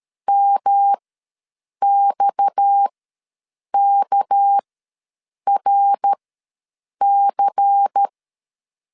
Morse code, heard as a keyed tone, about 13 wpm, spaced as for 7 wpm, 800 Hz. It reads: MXKRC
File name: LEZIONI / 6 PKX0 / 6 PKX ws 12wpm 3106